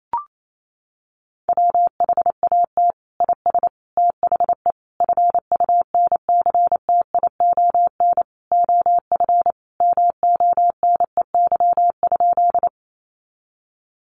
Morse code, heard as a keyed tone, about 28 wpm, 700 Hz. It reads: WHAT IS THE FUNCTION OF MONEY?
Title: Sentences from Top 1100 Words 28wpm